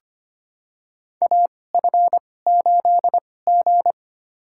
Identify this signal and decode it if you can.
Morse code, heard as a keyed tone, about 25 wpm, spaced as for 18 wpm, 700 Hz. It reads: AF8G
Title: USA Call Signs - 4 Characters - 2x Character Spacing - 25wpm